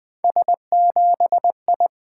Morse code, heard as a keyed tone, about 20 wpm, 700 Hz. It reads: S7I
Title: Three Letter-Number Combinations - 20wpm